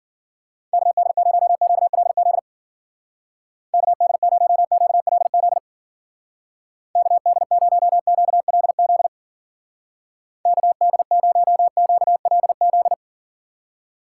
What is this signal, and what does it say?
Morse code, heard as a keyed tone, about 53 wpm, 700 Hz. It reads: KD0QLZ KD0QLZ KD0QLZ KD0QLZ